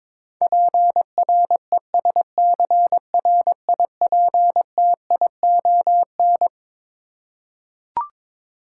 Morse code, heard as a keyed tone, about 22 wpm, 700 Hz. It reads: PRESCRIPTION
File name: practice / 22wpm / Twelve Letter Words - 22wpm